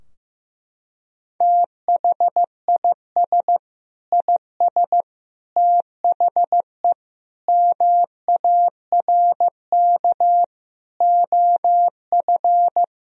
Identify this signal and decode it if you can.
Morse code, heard as a keyed tone, about 15 wpm, 700 Hz. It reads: THIS IS THE MARK OF
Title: Sentences from Top 1000 Words - Review Part 3 - 15wpm